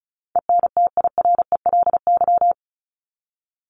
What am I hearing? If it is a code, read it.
Morse code, heard as a keyed tone, about 34 wpm, 700 Hz. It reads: ENTIRELY
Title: LY Suffix - Top 100 Words with Base Words 35wpm